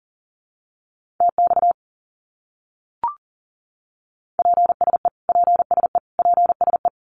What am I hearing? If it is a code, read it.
Morse code, heard as a keyed tone, about 40 wpm, 700 Hz. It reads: TX PSE PSE PSE
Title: Abbreviations Repeated 3x - 40wpm